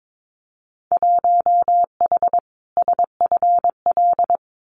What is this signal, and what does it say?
Morse code, heard as a keyed tone, about 22 wpm, 700 Hz. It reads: 1H SFL
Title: Winter Field Day Exchanges 22wpm